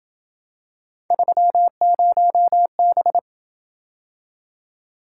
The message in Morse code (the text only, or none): 30B